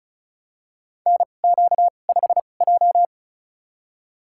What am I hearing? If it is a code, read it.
Morse code, heard as a keyed tone, about 36 wpm, spaced as for 25 wpm, 700 Hz. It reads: NQ5J
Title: USA Call Signs - 2x Character Spacing - 35wpm